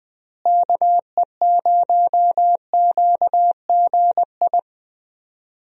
Morse code, heard as a keyed tone, about 20 wpm, 700 Hz. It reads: KE0QGI
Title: USA Call Signs - 6 Characters - Repeated 2x - 20wpm